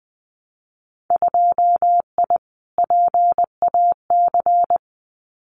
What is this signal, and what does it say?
Morse code, heard as a keyed tone, about 20 wpm, 700 Hz. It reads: 2I PAC
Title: Winter Field Day Exchanges Encore 20wpm